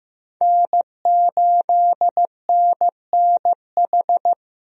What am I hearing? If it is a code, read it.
Morse code, heard as a keyed tone, about 15 wpm, 700 Hz. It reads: N8NNH
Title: USA Call Signs - 5 Characters - Repeated 2x - 15wpm